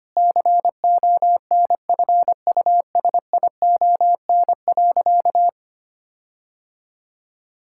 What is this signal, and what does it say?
Morse code, heard as a keyed tone, about 25 wpm, 700 Hz. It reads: CONFUSION.
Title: Sentences from Top 1600 Words 25wpm